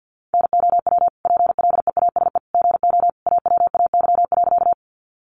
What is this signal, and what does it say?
Morse code, heard as a keyed tone, about 50 wpm, 700 Hz. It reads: NOW PLEASE GO AWAY.
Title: Sentences from Top 1500 Words 50wpm